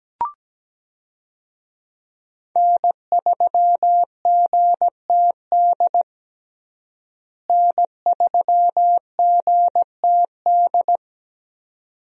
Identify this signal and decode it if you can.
Morse code, heard as a keyed tone, about 17 wpm, 700 Hz. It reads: N3GTD N3GTD
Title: USA Call Signs - 5 Characters - Repeated 2x - 17wpm